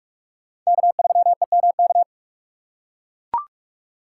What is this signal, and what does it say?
Morse code, heard as a keyed tone, about 42 wpm, 690 Hz. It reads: K3EMK
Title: USA Call Signs - 5 Characters - 6x Character Spacing - 45wpm